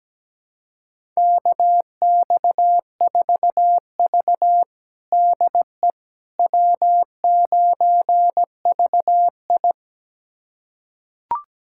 Morse code, heard as a keyed tone, about 17 wpm, 700 Hz. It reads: KX4V DE W9VI
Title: USA Callsign DE Callsign - 4 Characters - 17wpm